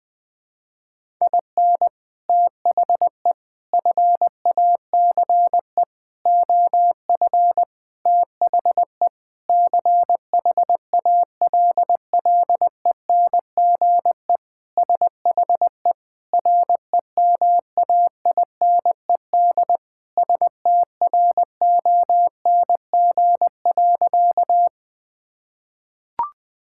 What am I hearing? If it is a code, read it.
Morse code, heard as a keyed tone, about 20 wpm, 700 Hz. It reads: IN THE FACE OF THE CHALLENGE SHE REMAINED STRONG.
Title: Sentences with 5-Word Phrases - 20wpm